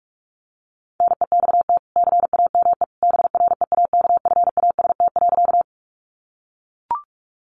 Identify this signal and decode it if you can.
Morse code, heard as a keyed tone, about 44 wpm, 700 Hz. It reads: NEXT CAME BREAKFAST.
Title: Sentences from Top 1500 Words 45wpm